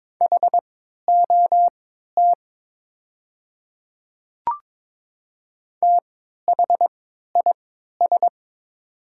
Morse code, heard as a keyed tone, about 22 wpm, spaced as for 12 wpm, 700 Hz. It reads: HOT THIS
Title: 3x Character Spacing - Top 100 Words - 22wpm